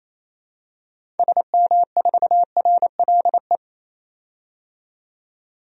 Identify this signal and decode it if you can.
Morse code, heard as a keyed tone, about 28 wpm, 700 Hz. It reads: SM4RLE
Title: DXCC-100 Call Signs - 6 Characters - 28wpm